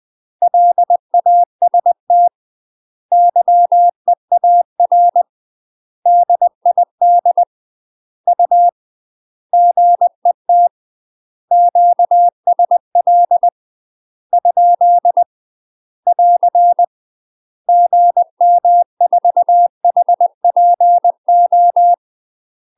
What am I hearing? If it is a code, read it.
Morse code, heard as a keyed tone, about 20 wpm, 700 Hz. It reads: LAST YEAR DID U GET QSL ? <AR> GM4HPO
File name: UK MASTER QSOs TRAINING / 20WPM / UK MASTER QSOs 0073